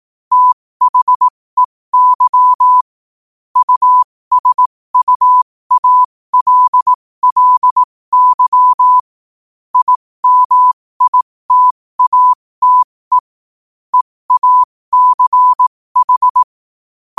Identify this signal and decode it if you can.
Morse code, heard as a keyed tone, about 18 wpm, spaced as for 15 wpm, 1000 Hz. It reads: THEY USUALLY IMITATE EACH
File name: qotdfiles / 15wpm